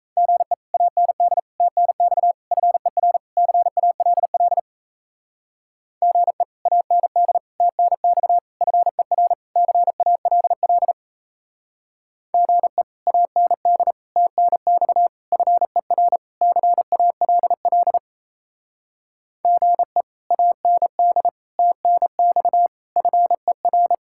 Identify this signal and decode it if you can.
Morse code, heard as a keyed tone, about 36 wpm, 700 Hz. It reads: GE AND TNX FER CALL GE AND TNX FER CALL GE AND TNX FER CALL GE AND TNX FER